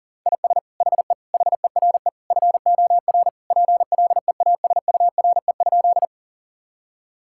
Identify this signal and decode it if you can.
Morse code, heard as a keyed tone, about 40 wpm, 700 Hz. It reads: IS HE HERE FOR PLEASURE?
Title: Sentences from Top 1600 Words 40wpm